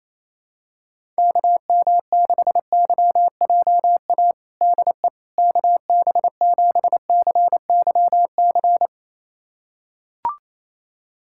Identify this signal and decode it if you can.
Morse code, heard as a keyed tone, about 28 wpm, 700 Hz. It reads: KM6YJA DE KB7CYC
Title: USA Callsign DE Callsign - 6 Characters - 28wpm